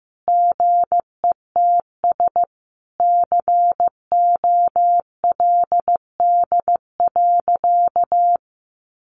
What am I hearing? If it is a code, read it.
Morse code, heard as a keyed tone, about 15 wpm, 700 Hz. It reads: GETS COLD.
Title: Sentences from Top 1300 Words 15wpm